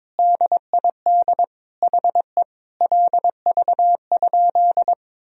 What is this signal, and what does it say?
Morse code, heard as a keyed tone, about 22 wpm, 700 Hz. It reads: DID HE LV?
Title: Sentences from Top 400 Words and Abbreviations 22wpm